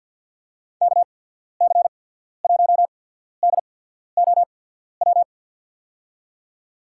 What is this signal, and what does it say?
Morse code, heard as a keyed tone, about 52 wpm, spaced as for 14 wpm, 700 Hz. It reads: KC1DOW
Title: USA Call Signs - 6 Characters - 8x Character Spacing - 50wpm